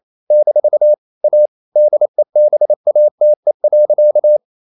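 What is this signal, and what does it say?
Morse code, heard as a keyed tone, about 28 wpm, 600 Hz. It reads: - A DEBATE.
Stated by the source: Morse code practice at 28 wpm, text of Ken Follett's novel Never in English, from CW med Gnister.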